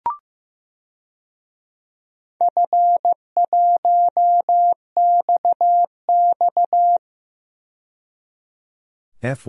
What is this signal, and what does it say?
Morse code, heard as a keyed tone, about 15 wpm, 700 Hz. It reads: F1XX
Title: DXCC-100 Call Signs - 4 Characters - 15wpm